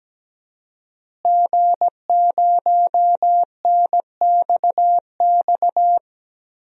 Morse code, heard as a keyed tone, about 17 wpm, 700 Hz. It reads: G0NXX
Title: DXCC-100 Call Signs - 5 Characters - Repeated 2x - 17wpm